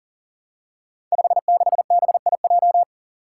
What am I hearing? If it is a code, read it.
Morse code, heard as a keyed tone, about 40 wpm, 700 Hz. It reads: 56BIJ
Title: Five Challenging Letter-Number Combinations 40wpm